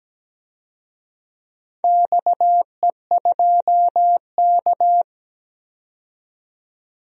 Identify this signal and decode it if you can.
Morse code, heard as a keyed tone, about 17 wpm, 700 Hz. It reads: XE2K
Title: DXCC-100 Call Signs - 4 Characters - 17wpm